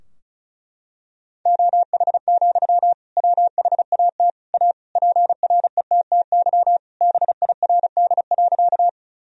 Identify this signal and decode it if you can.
Morse code, heard as a keyed tone, about 36 wpm, 700 Hz. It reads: OH, WHAT A PRETTY BIRD.
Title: Sentences from Top 1000 Words - Review Part 2 - 35wpm